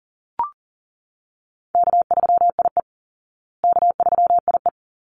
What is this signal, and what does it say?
Morse code, heard as a keyed tone, about 40 wpm, 700 Hz. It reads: K3IE K3IE
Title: Easy Call Signs Repeated 2x - 40wpm